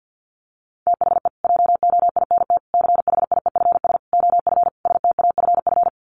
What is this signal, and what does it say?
Morse code, heard as a keyed tone, about 48 wpm, 700 Hz. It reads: THE JOINT CHIEFS OF STAFF
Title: Sets of 5 Words Encore2 50wpm